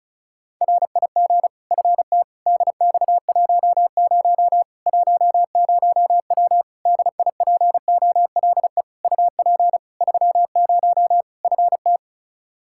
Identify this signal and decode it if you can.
Morse code, heard as a keyed tone, about 36 wpm, 700 Hz. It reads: RIG FT DX10 10W DIPOLE UP 30 FT
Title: Practice QSO - Exchange 2 - Info - 35wpm